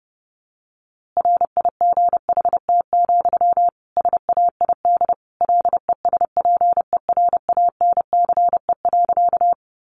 Morse code, heard as a keyed tone, about 30 wpm, 700 Hz. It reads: RIGHT, SAID LESPERANCE.
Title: A Sound Of Thunder 30wpm